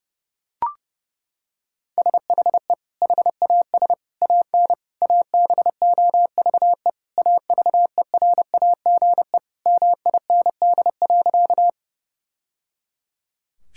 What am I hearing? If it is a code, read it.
Morse code, heard as a keyed tone, about 30 wpm, 700 Hz. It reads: SHE HAS AN ABOVE AVERAGE MIND.
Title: Sentences from Top 1000 Words - Review Part 6 - 30wpm